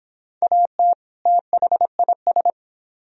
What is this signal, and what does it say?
Morse code, heard as a keyed tone, about 25 wpm, 700 Hz. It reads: AT THIS